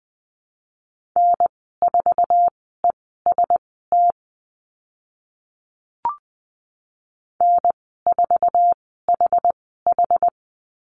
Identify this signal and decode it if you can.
Morse code, heard as a keyed tone, about 20 wpm, spaced as for 14 wpm, 700 Hz. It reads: N4EST N4HH
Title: 069 Lesson 12g - Letter d - Call Signs 20wpm